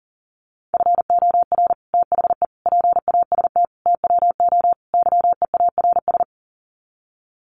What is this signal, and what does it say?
Morse code, heard as a keyed tone, about 40 wpm, 700 Hz. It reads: FOR THE PAST TWO YEARS
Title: Sets of 5 Words Encore2 40wpm